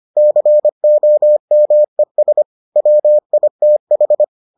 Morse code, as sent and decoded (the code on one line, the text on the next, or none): -.-. --- -- . ... / .-- .. - ....
COMES WITH